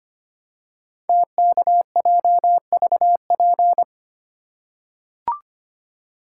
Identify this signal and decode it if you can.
Morse code, heard as a keyed tone, about 25 wpm, 700 Hz. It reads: TKJVP
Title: Code Groups - 5 Letters - 25wpm